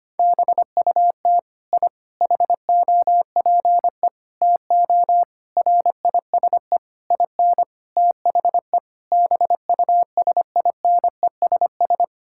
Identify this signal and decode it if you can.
Morse code, heard as a keyed tone, about 25 wpm, 700 Hz. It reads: BUT I HOPE TO RISE IN THE BUSINESS
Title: Sentences from Top 1000 Words - Review Part 6 - 25wpm